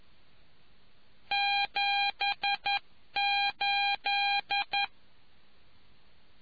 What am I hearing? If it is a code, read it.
Morse code, heard as a keyed tone, about 11 wpm, 800 Hz. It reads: 78